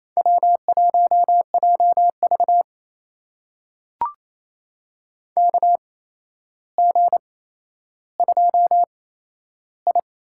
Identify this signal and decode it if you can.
Morse code, heard as a keyed tone, about 28 wpm, spaced as for 12 wpm, 700 Hz. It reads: W1JV KG2I
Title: USA Call Signs - 4 Characters - 8x Character Spacing - 28wpm